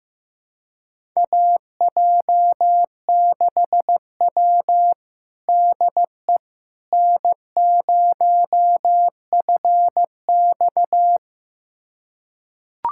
Morse code, heard as a keyed tone, about 15 wpm, 700 Hz. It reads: AJ6W DE N0FX
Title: USA Callsign DE Callsign - 4 Characters - 15wpm